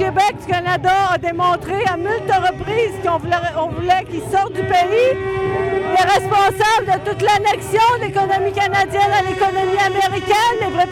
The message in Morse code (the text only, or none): none